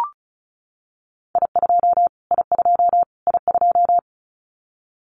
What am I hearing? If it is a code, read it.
Morse code, heard as a keyed tone, about 34 wpm, 700 Hz. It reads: I2 I2 I2
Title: Callsign Prefixes for Top 25 DXCC - Repeated 3x - 35wpm